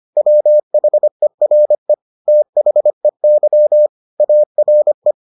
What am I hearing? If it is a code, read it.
Morse code, heard as a keyed tone, about 25 wpm, 600 Hz. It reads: WHERE THEY ARE